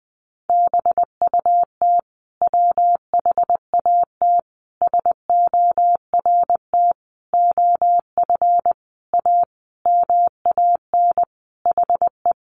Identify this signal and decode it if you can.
Morse code, heard as a keyed tone, about 19 wpm, 700 Hz. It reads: BUT WHAT SORT OF A MAN HE
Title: Sentences from Top 1300 Words 20wpm